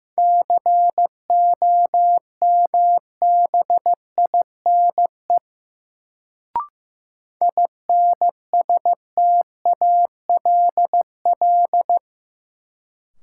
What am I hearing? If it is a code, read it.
Morse code, heard as a keyed tone, about 15 wpm, 700 Hz. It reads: COMBINE INSTALL
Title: Sentences from Top 1600 Words - New Word Drill - 15wpm